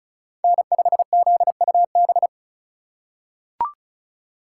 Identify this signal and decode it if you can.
Morse code, heard as a keyed tone, about 36 wpm, 700 Hz. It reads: N5ZUB
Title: USA Call Signs - 5 Characters - 2x Character Spacing - 35wpm